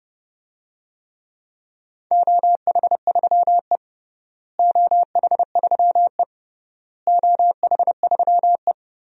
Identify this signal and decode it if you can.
Morse code, heard as a keyed tone, about 32 wpm, 700 Hz. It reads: OH3E OH3E OH3E
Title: DXCC-100 Call Signs - 4 Characters - Repeated 3x - 30wpm